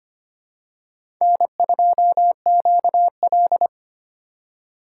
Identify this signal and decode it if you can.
Morse code, heard as a keyed tone, about 25 wpm, 700 Hz. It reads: N2QL
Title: USA Call Signs - 4 Characters - 3x Character Spacing - 25wpm